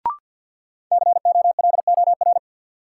Morse code, heard as a keyed tone, about 48 wpm, 700 Hz. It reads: COLOR
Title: Sentences from Top 1600 Words - New Word Drill Alt - 50wpm